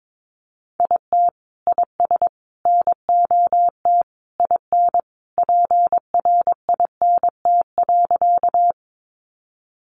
Morse code, heard as a keyed tone, about 22 wpm, 700 Hz. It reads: IT IS NOT IN PRINT.
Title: Sentences from Top 1400 Words 22wpm